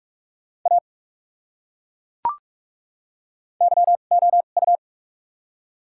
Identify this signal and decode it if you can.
Morse code, heard as a keyed tone, about 42 wpm, spaced as for 31 wpm, 700 Hz. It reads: A YOU
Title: 2x Character Spacing - Top 100 Words - 45wpm